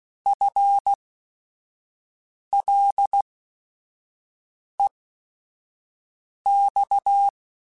Morse code, heard as a keyed tone, about 16 wpm, spaced as for 5 wpm, 800 Hz. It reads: FLEX
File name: FL / FL mp3s / Fast FL @5